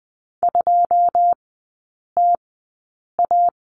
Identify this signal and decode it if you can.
Morse code, heard as a keyed tone, about 20 wpm, spaced as for 8 wpm, 700 Hz. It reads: 2TA
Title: 093 Lesson 16e - Letter c - Three Characters 20wpm